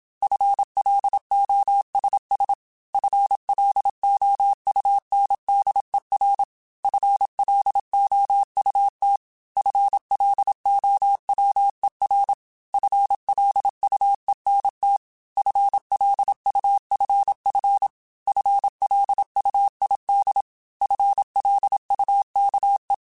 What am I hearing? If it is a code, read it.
Morse code, heard as a keyed tone, about 27 wpm, 800 Hz. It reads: FLOSS FLOUNDER FLOUT FLOWER FLUENT FLUFF FLUID FLUKE